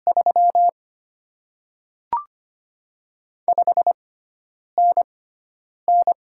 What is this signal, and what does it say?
Morse code, heard as a keyed tone, about 25 wpm, spaced as for 9 wpm, 700 Hz. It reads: 3 5NN
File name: Top 20 QSO Elements - 6x Character Spacing - 25wpm